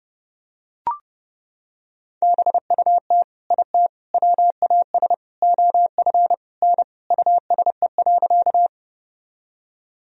Morse code, heard as a keyed tone, about 30 wpm, 700 Hz. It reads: BUT IT WAS OF N USE.
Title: Sentences from Top 100 Words and Abbreviations 30wpm